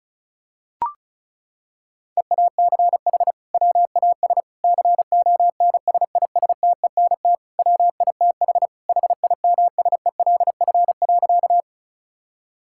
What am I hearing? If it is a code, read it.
Morse code, heard as a keyed tone, about 36 wpm, 700 Hz. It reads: EACH WAS CONSISTENT WITH HIMSELF.
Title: Sentences from Top 1600 Words 35wpm